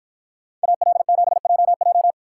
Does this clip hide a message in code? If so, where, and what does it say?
Yes: Morse code from 0.6 to 2.1 s, heard as a keyed tone, about 53 wpm, 700 Hz. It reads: AL7JJ